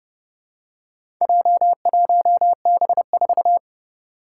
Morse code, heard as a keyed tone, about 30 wpm, 700 Hz. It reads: J1B4
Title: Four Challenging Letter-Number Combinations 30wpm